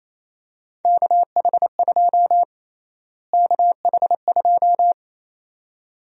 Morse code, heard as a keyed tone, about 28 wpm, 700 Hz. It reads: KH2 KH2
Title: USA Call Sign Prefixes - Repeated 2x - 28wpm